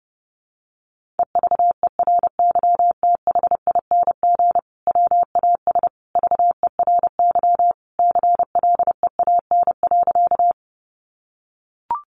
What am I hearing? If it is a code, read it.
Morse code, heard as a keyed tone, about 30 wpm, 700 Hz. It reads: EVERYTHING WAS VERY CLEAN.
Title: Sentences from Top 1100 Words 30wpm